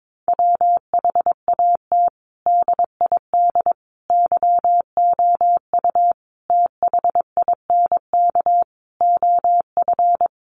Morse code, heard as a keyed tone, about 22 wpm, 700 Hz. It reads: WHAT DID YOU THINK OF
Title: Sets of 5 Words Encore2 22wpm